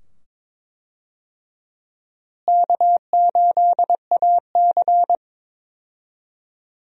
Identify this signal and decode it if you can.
Morse code, heard as a keyed tone, about 22 wpm, 700 Hz. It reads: K8AC
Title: USA Call Signs - 4 Characters - Repeated 2x - 22wpm